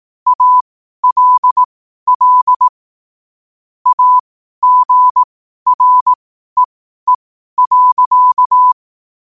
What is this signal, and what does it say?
Morse code, heard as a keyed tone, about 18 wpm, spaced as for 12 wpm, 1000 Hz. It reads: ALL AGREE.
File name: qotdfiles / 12wpm